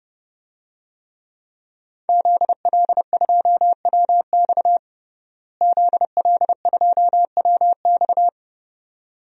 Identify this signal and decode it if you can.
Morse code, heard as a keyed tone, about 30 wpm, 700 Hz. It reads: ZL2WX ZL2WX
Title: DXCC-100 Call Signs - 5 Characters - Repeated 2x - 30wpm